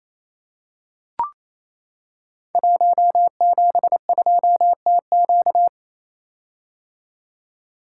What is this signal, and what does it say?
Morse code, heard as a keyed tone, about 28 wpm, 700 Hz. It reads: 172TQ
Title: Code Groups - 5 Letter:Numbers - 28wpm